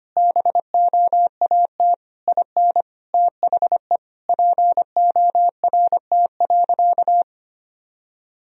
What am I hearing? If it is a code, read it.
Morse code, heard as a keyed tone, about 25 wpm, 700 Hz. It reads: BOAT IN THE PORT.
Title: Sentences from Top 1000 Words - Review Part 1 - 25wpm